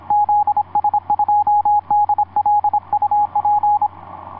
Morse code, heard as a keyed tone, about 26 wpm, 830 Hz, with radio noise. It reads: ZS2DL UP
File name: zs2dl ea2ok etc 12m cw 07mar2010 1637